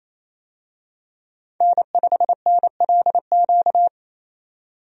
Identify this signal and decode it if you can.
Morse code, heard as a keyed tone, about 28 wpm, 700 Hz. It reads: N5NLQ
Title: USA Call Signs - 5 Characters - 28wpm